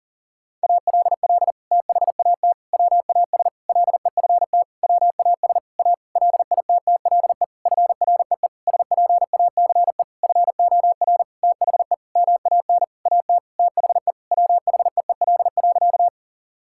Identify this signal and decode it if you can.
Morse code, heard as a keyed tone, about 40 wpm, 700 Hz. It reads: ALL THAT WAS LEFT WAS A LITTLE FREE SPACE FOR THE MAN AT THE WHEEL.
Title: Sentences from Top 1000 Words - Review Part 5 - 40wpm